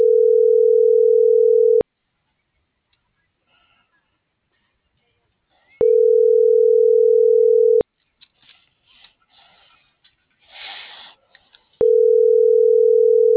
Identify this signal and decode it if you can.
Ringback tone (North America), 440 + 480 Hz.